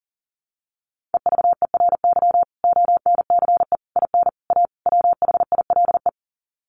Morse code, heard as a keyed tone, about 40 wpm, 700 Hz. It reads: EVERY ONCE IN A WHILE
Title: Sets of 5 Words 40wpm